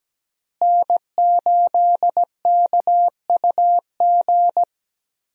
Morse code, heard as a keyed tone, about 17 wpm, 700 Hz. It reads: N8KUG
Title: USA Call Signs - 5 Characters - Repeated 3x - 17wpm